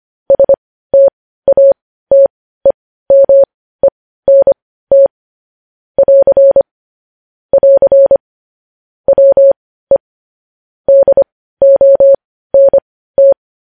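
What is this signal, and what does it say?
Morse code, heard as a keyed tone, about 25 wpm, spaced as for 15 wpm, 550 Hz. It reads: STATEMENT <AR> <AR> WE DONT
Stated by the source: News headlines in Morse code at 15 words per minute, characters sent at 25 WPM.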